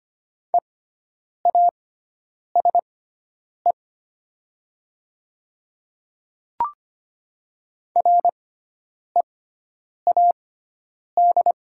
Morse code, heard as a keyed tone, about 25 wpm, spaced as for 9 wpm, 700 Hz. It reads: EASE READ
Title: 6x Character Spacing - Top 300 Words - 25wpm